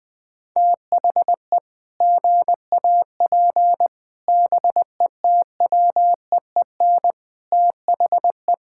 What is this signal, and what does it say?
Morse code, heard as a keyed tone, about 20 wpm, 700 Hz. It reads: THE GAP BETWEEN THE
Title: Sentences with 8-Word Phrases - 20wpm